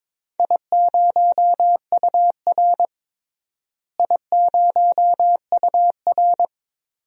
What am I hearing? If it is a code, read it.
Morse code, heard as a keyed tone, about 22 wpm, 700 Hz. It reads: I0UR I0UR
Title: DXCC-100 Call Signs - 4 Characters - Repeated 2x - 22wpm